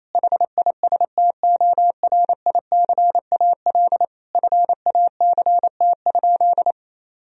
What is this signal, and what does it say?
Morse code, heard as a keyed tone, about 28 wpm, 700 Hz. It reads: HISTORICAL FACT?
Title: Sentences from Top 1000 Words - Review Part 4 - 28wpm